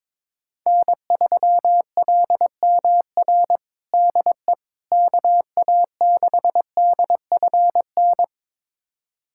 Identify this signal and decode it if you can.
Morse code, heard as a keyed tone, about 22 wpm, 700 Hz. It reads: N3LMR DE KA6DFN
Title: USA Callsign DE Callsign 22wpm